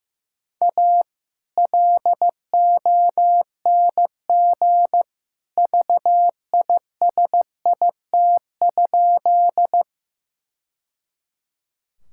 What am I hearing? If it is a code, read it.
Morse code, heard as a keyed tone, about 15 wpm, 700 Hz. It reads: A LONG VISIT?
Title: Sentences from Top 1000 Words - Review Part 6 - 15wpm